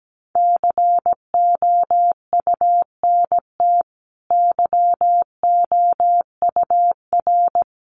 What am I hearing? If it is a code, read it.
Morse code, heard as a keyed tone, about 17 wpm, 700 Hz. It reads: COUNT YOUR
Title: Sentences from Top 1500 Words 17wpm